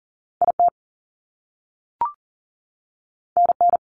Morse code, heard as a keyed tone, about 40 wpm, 700 Hz. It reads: IT NN
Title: Top Two Letter Combinations in Words 40wpm